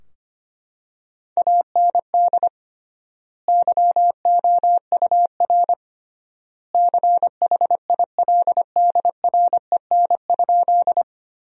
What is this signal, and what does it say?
Morse code, heard as a keyed tone, about 25 wpm, 700 Hz. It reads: AND YOUR CHILDREN?
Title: 3x Word-Spacing - Sentences from Top 300 Words - 25wpm